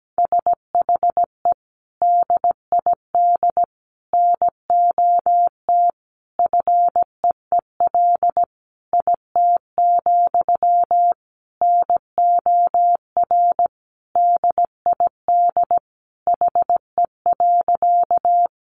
Morse code, heard as a keyed tone, about 17 wpm, 700 Hz. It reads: SHE DID NOT FEEL IT, NOR DID HE.
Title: Sentences from Top 1500 Words 17wpm